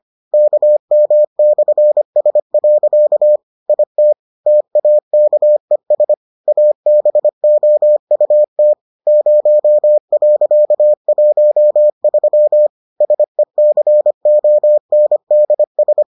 Morse code, heard as a keyed tone, about 25 wpm, 600 Hz. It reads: KM/S. IT TAKES ABOUT 0.13 SECONDS